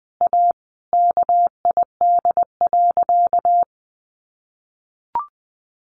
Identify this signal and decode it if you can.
Morse code, heard as a keyed tone, about 20 wpm, 700 Hz. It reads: A KID.
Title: A Sound Of Thunder 20wpm